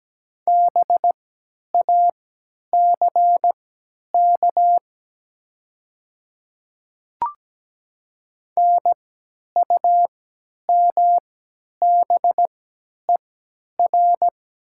Morse code, heard as a keyed tone, about 17 wpm, spaced as for 10 wpm, 700 Hz. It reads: BACK NUMBER
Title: 3x Character Spacing - Top 200 Words - 17wpm